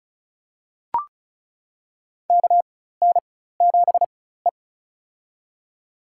Morse code, heard as a keyed tone, about 36 wpm, spaced as for 16 wpm, 700 Hz. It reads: KN7E
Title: USA Call Signs - 4x Character Spacing - 35wpm